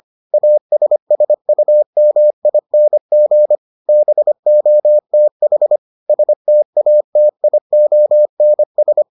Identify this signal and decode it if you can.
Morse code, heard as a keyed tone, about 25 wpm, 600 Hz. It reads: ASSUMING BOTH STATIONS